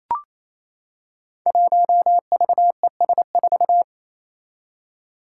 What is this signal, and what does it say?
Morse code, heard as a keyed tone, about 28 wpm, 700 Hz. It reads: 1VES4
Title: Five Challenging Letter-Number Combinations 28wpm